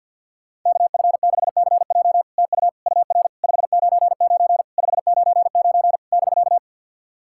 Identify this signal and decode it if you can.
Morse code, heard as a keyed tone, about 50 wpm, 700 Hz. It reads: KF6CJ TU UR 599 599 <BK>
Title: SOTA QSO Practice 50wpm